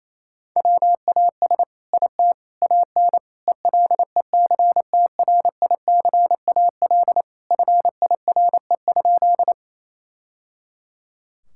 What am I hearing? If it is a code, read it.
Morse code, heard as a keyed tone, about 27 wpm, 700 Hz. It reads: WAS IT AN ELECTRICAL FIRE?
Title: Sentences from Top 1000 Words - Review Part 6 - 28wpm